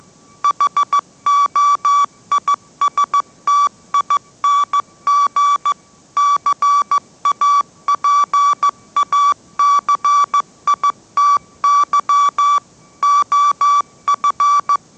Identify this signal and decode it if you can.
Morse code, heard as a keyed tone, about 15 wpm, 1150 Hz. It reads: HOISTING CAPACITY OF
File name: mrx / mrx065 16wpm